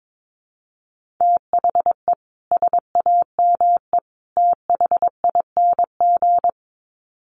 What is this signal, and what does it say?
Morse code, heard as a keyed tone, about 22 wpm, 700 Hz. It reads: THE SAME THING
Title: Sets of 3 Words 22wpm